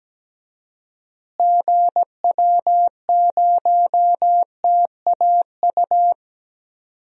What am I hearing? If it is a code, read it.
Morse code, heard as a keyed tone, about 17 wpm, 700 Hz. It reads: GW0TAU
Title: DXCC-100 Call Signs - 6 Characters - Repeated 3x - 17wpm